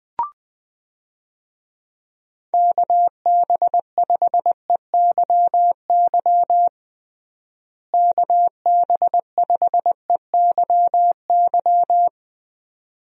Morse code, heard as a keyed tone, about 20 wpm, 700 Hz. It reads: KB5EYY KB5EYY
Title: USA Call Signs - 6 Characters - Repeated 2x - 20wpm